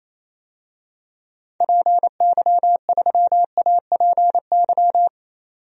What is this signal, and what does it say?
Morse code, heard as a keyed tone, about 28 wpm, 700 Hz. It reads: PY3APY
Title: DXCC-100 Call Signs - 6 Characters - 28wpm